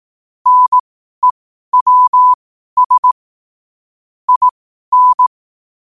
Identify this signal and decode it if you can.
Morse code, heard as a keyed tone, about 18 wpm, spaced as for 12 wpm, 1000 Hz. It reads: NEWS IN
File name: qotdfiles / 12wpm